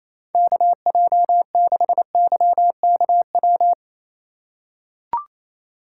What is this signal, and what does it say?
Morse code, heard as a keyed tone, about 28 wpm, 700 Hz. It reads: KJ6YKW
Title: USA Call Signs - 6 Characters - 4x Character Spacing - 28wpm